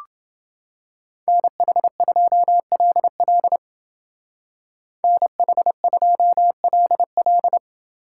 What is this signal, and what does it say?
Morse code, heard as a keyed tone, about 28 wpm, 700 Hz. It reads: NH2LL NH2LL